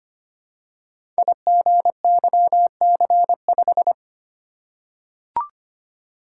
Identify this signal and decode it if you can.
Morse code, heard as a keyed tone, about 25 wpm, 700 Hz. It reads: IGYC5
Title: Code Groups - 5 Letter:Numbers - 25wpm